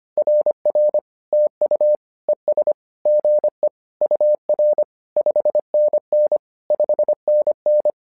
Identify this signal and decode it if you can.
Morse code, heard as a keyed tone, about 25 wpm, 600 Hz. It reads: RR TU ES GE UR 5NN 5NN
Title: SOTA QSO Practice 25wpm